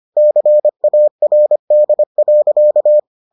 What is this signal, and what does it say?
Morse code, heard as a keyed tone, about 25 wpm, 600 Hz. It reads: CARD.